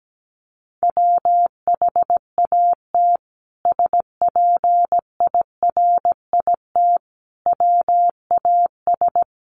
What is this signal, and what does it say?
Morse code, heard as a keyed tone, about 17 wpm, 700 Hz. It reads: WHAT SPIRIT WAS
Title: Sentences from Top 1400 Words 17wpm